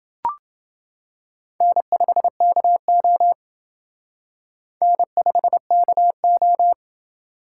Morse code, heard as a keyed tone, about 28 wpm, 700 Hz. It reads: N5KO N5KO